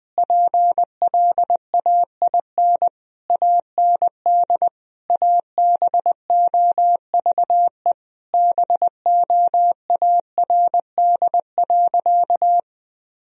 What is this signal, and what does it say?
Morse code, heard as a keyed tone, about 20 wpm, 700 Hz. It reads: PLAIN AND ABOVE BOARD.
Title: Sentences from Top 1000 Words - Review Part 4 - 20wpm